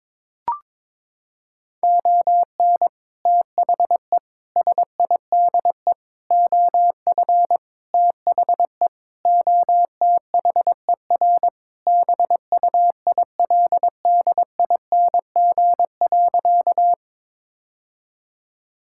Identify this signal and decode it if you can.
Morse code, heard as a keyed tone, about 21 wpm, 700 Hz. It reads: ON THE SIDE OF THE OTHER BUILDING.
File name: practice / 22wpm / Sentences with 3-Word Phrases - 22wpm